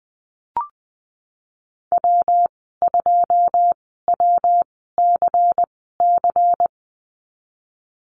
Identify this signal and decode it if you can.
Morse code, heard as a keyed tone, about 20 wpm, spaced as for 14 wpm, 700 Hz. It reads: W2WCC
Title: 186 Lesson 30g - Letter b - Call Signs 20wpm